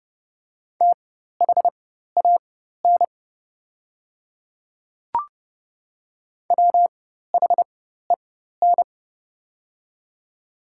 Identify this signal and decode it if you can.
Morse code, heard as a keyed tone, about 30 wpm, spaced as for 14 wpm, 690 Hz. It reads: THAN WHEN
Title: 4x Character Spacing - Top 100 Words - 30wpm